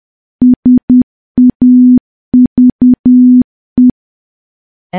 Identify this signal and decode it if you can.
Morse code, heard as a keyed tone, about 10 wpm, 250 Hz. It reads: SAVE
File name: morse / morse-training-10WPM